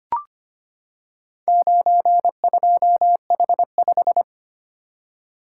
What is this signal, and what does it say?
Morse code, heard as a keyed tone, about 25 wpm, 700 Hz. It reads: 92H5
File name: practice / 25wpm / Four Letter-Number Combinations - 25wpm